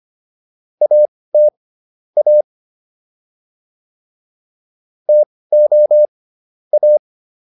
Morse code, heard as a keyed tone, about 25 wpm, spaced as for 18 wpm, 600 Hz. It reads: AT A TO A